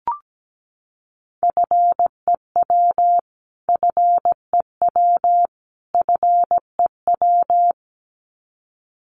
Morse code, heard as a keyed tone, about 17 wpm, 700 Hz. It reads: FEW FEW FEW
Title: Top 300 Words Repeated 17wpm